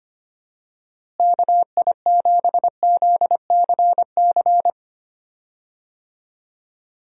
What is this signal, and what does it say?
Morse code, heard as a keyed tone, about 25 wpm, 700 Hz. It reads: KI7ZCC
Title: USA Call Signs - 6 Characters - Repeated 2x - 25wpm